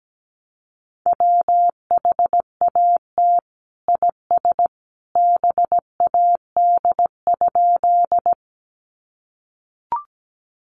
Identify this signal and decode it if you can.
Morse code, heard as a keyed tone, about 17 wpm, 700 Hz. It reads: WHAT IS BAD?
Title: Sentences from Top 700 Words 17wpm